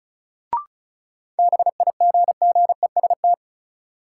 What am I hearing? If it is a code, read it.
Morse code, heard as a keyed tone, about 36 wpm, 700 Hz. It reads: BIGGEST
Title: Sets of 4 Words - New Word Drill - 35wpm